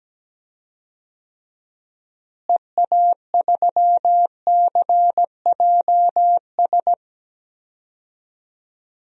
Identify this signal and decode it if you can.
Morse code, heard as a keyed tone, about 17 wpm, 700 Hz. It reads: EA3CJS
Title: DXCC-100 Call Signs - 6 Characters - Repeated 2x - 17wpm